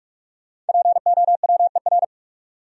Morse code, heard as a keyed tone, about 58 wpm, 690 Hz. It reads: POWER